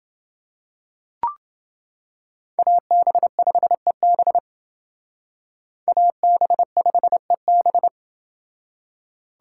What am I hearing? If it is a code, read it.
Morse code, heard as a keyed tone, about 28 wpm, 700 Hz. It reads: AB5EB AB5EB